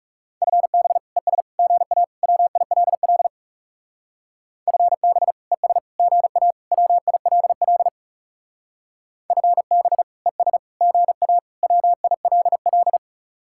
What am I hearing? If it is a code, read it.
Morse code, heard as a keyed tone, about 36 wpm, 700 Hz. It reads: FB ES GA WILL FB ES GA WILL FB ES GA WILL
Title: Speed-Racing Common QSO Phrases - 30wpm